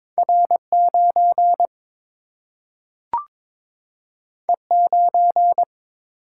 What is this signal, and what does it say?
Morse code, heard as a keyed tone, about 22 wpm, 700 Hz. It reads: R 9 E 9
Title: Single-Letter Single-Number 22wpm